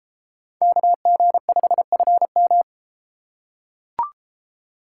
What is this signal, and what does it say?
Morse code, heard as a keyed tone, about 32 wpm, 700 Hz. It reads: KG5FM